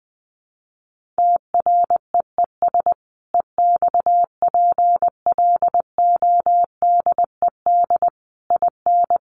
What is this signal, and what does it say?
Morse code, heard as a keyed tone, about 20 wpm, 700 Hz. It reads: TREES EXPLODED IN
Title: A Sound Of Thunder 20wpm